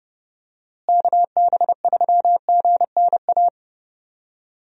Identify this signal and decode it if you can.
Morse code, heard as a keyed tone, about 32 wpm, 700 Hz. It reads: KB3GNA
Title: USA Call Signs - 6 Characters - Repeated 2x - 30wpm